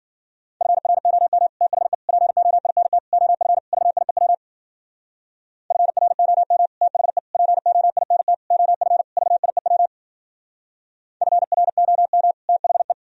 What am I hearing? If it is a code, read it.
Morse code, heard as a keyed tone, about 53 wpm, 700 Hz. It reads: FROM THE POINT OF VIEW FROM THE POINT OF VIEW FROM THE